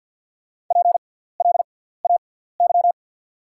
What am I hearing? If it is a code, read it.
Morse code, heard as a keyed tone, about 48 wpm, spaced as for 17 wpm, 690 Hz. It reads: PLAY
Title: 6x Character Spacing - Top 200 Words - 50wpm